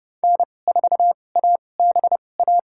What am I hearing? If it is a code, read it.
Morse code, heard as a keyed tone, about 30 wpm, spaced as for 22 wpm, 700 Hz. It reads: N4ABA
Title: USA Call Signs - 5 Characters - 2x Character Spacing - 30wpm